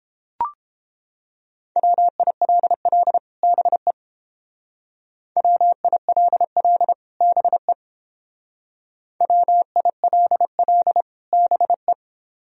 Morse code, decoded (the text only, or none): WILL BE WILL BE WILL BE